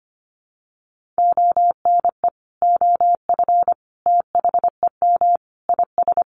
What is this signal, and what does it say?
Morse code, heard as a keyed tone, about 25 wpm, 700 Hz. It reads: ONE OF THEM IS
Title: Additional Easy Phrases 25wpm